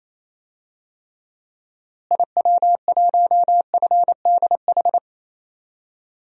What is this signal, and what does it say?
Morse code, heard as a keyed tone, about 28 wpm, 700 Hz. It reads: IW1FDH
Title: DXCC-100 Call Signs - 6 Characters - Repeated 3x - 28wpm